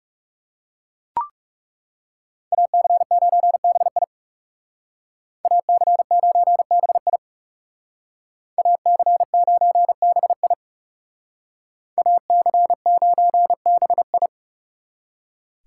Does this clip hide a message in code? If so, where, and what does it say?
Yes: Morse code from 2.5 to 14.3 s, heard as a keyed tone, about 38 wpm, 700 Hz. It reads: AC9BI AC9BI AC9BI AC9BI